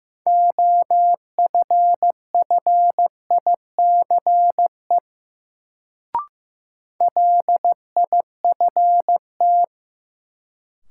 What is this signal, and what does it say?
Morse code, heard as a keyed tone, about 15 wpm, 700 Hz. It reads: OFFICE LIFT
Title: Sentences from Top 1000 Words - New Word Drill - 15wpm